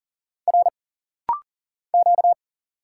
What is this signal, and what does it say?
Morse code, heard as a keyed tone, about 39 wpm, 700 Hz. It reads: R Q
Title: Single Letter-Number - Rapid-Fire 40wpm